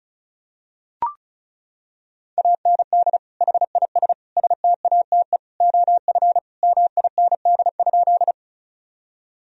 Morse code, heard as a keyed tone, about 34 wpm, 700 Hz. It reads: AND HIS STATE OF MIND?
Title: Sentences from Top 1000 Words - Review Part 2 - 35wpm